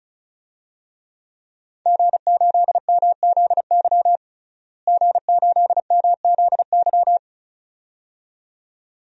Morse code, heard as a keyed tone, about 36 wpm, 700 Hz. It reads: G8MZY G8MZY
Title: DXCC-100 Call Signs - 5 Characters - Repeated 2x - 35wpm